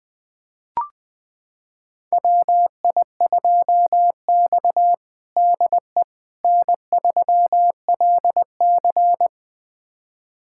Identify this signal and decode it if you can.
Morse code, heard as a keyed tone, about 20 wpm, 700 Hz. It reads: WI2X DE N3LC
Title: USA Callsign DE Callsign - 4 Characters - 20wpm